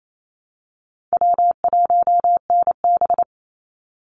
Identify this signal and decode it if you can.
Morse code, heard as a keyed tone, about 28 wpm, 700 Hz. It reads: W1NB
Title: USA Call Signs 28wpm